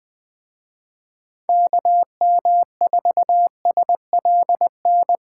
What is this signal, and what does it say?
Morse code, heard as a keyed tone, about 24 wpm, 700 Hz. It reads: KM4SLN
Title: Speed Racing - USA 2x3 Call Signs 20wpm